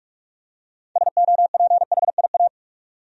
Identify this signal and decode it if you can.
Morse code, heard as a keyed tone, about 46 wpm, 700 Hz. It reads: SOPHIA
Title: USA Common Names - 45wpm